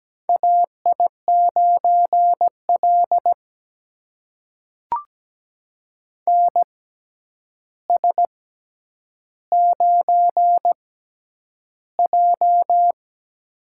Morse code, heard as a keyed tone, about 17 wpm, spaced as for 9 wpm, 700 Hz. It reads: AI9L NS9J
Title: USA Call Signs - 4 Characters - 6x Character Spacing - 17wpm